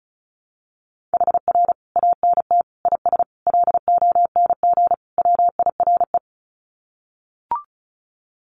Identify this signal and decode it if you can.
Morse code, heard as a keyed tone, about 34 wpm, 700 Hz. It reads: HR ANT IS LONG WIRE
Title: Common QSO Phrases 35wpm